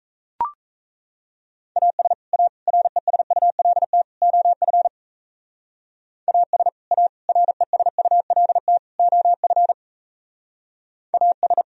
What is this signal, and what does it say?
Morse code, heard as a keyed tone, about 34 wpm, 700 Hz. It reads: AS A RESULT OF AS A RESULT OF AS
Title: Speed-Racing - Sets of 4 Words 28wpm